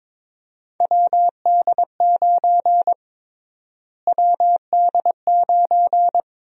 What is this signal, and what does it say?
Morse code, heard as a keyed tone, about 23 wpm, 700 Hz. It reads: WD9 WD9
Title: USA Call Sign Prefixes - Repeated 2x - 22wpm